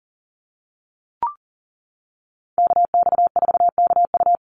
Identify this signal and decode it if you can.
Morse code, heard as a keyed tone, about 40 wpm, 700 Hz. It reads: KX4KU
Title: Call Signs - 40wpm